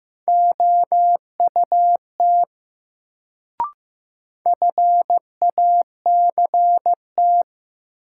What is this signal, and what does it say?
Morse code, heard as a keyed tone, about 15 wpm, 700 Hz. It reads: OUT FACT
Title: Sets of 2 Words - New Word Drill - 15wpm